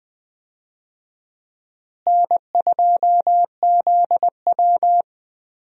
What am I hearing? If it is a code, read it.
Morse code, heard as a keyed tone, about 20 wpm, 700 Hz. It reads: N2ZW
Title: USA Call Signs - 4 Characters - Repeated 2x - 20wpm